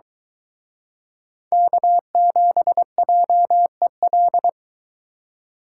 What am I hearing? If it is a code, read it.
Morse code, heard as a keyed tone, about 24 wpm, 700 Hz. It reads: K7JEL